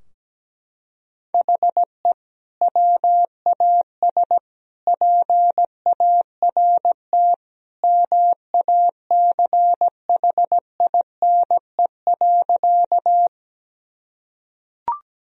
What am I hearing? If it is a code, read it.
Morse code, heard as a keyed tone, about 17 wpm, 700 Hz. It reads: HE WAS PART MACHINE.
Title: Sentences from Top 1000 Words - Review Part 2 - 17wpm